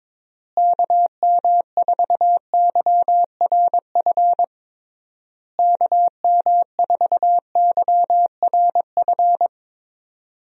Morse code, heard as a keyed tone, about 21 wpm, 700 Hz. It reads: KM4YRF KM4YRF
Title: USA Call Signs - 6 Characters - Repeated 3x - 22wpm